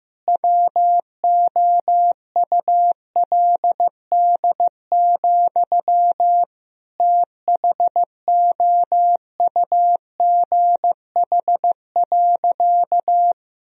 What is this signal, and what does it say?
Morse code, heard as a keyed tone, about 16 wpm, 700 Hz. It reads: WOULD, THOUGH.
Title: Sentences from Top 1000 Words - Review Part 2 - 15wpm